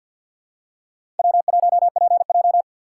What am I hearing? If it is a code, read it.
Morse code, heard as a keyed tone, about 53 wpm, 700 Hz. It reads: W1PJ
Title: USA Call Signs - 4 Characters - 6x Character Spacing - 50wpm